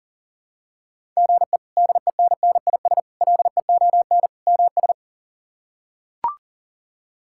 Morse code, heard as a keyed tone, about 40 wpm, 700 Hz. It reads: GE DENNIS LEON MS
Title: K1USN SST QSO Practice - 40wpm